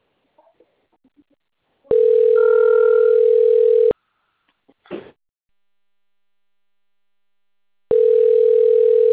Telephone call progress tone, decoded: ringback tone

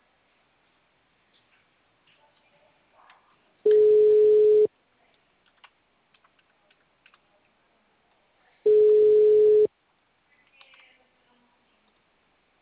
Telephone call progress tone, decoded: ringback tone

